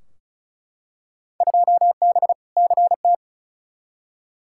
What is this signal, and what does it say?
Morse code, heard as a keyed tone, about 34 wpm, 700 Hz. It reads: 2B CT
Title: Field Day Exchanges Encore - 35wpm